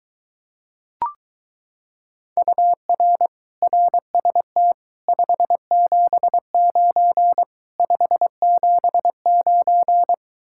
Morse code, heard as a keyed tone, about 23 wpm, 700 Hz. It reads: UR RST 579 579